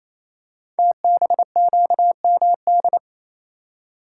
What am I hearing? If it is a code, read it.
Morse code, heard as a keyed tone, about 28 wpm, 700 Hz. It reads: TBQMD
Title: Code Groups - 5 Letters - 28wpm